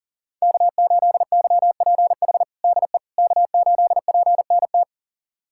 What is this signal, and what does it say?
Morse code, heard as a keyed tone, about 40 wpm, 700 Hz. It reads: K8YPH DE K8PNT
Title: USA Callsign DE Callsign - 5 Characters - 40wpm